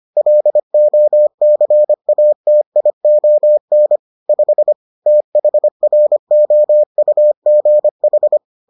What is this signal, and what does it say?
Morse code, heard as a keyed tone, about 25 wpm, 600 Hz. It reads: LOCATION 5 THROUGH